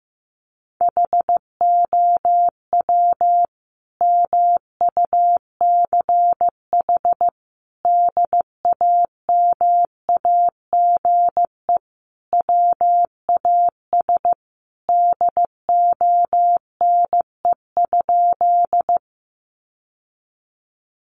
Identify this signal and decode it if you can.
Morse code, heard as a keyed tone, about 15 wpm, 700 Hz. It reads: HOW MUCH DAMAGE WAS DONE?
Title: Sentences from Top 1100 Words 15wpm